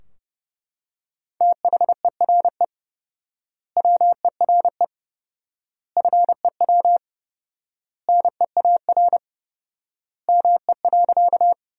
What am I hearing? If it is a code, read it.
Morse code, heard as a keyed tone, about 30 wpm, 700 Hz. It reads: THERE WERE FEW NEAR ME.
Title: 4x Word-Spacing - Sentences from Top 300 Words - 30wpm